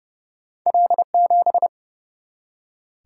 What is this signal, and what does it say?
Morse code, heard as a keyed tone, about 30 wpm, 700 Hz. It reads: L7
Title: Single-Letter Single-Number 30wpm